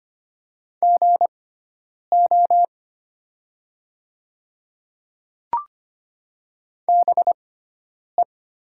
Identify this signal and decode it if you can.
Morse code, heard as a keyed tone, about 25 wpm, spaced as for 9 wpm, 700 Hz. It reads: GO BE
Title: 6x Character Spacing - Top 300 Words - 25wpm